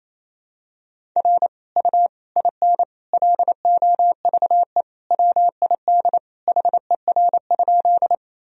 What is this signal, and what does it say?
Morse code, heard as a keyed tone, about 28 wpm, 700 Hz. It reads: R U IN LOVE WID HER?
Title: Sentences from Top 400 Words and Abbreviations 28wpm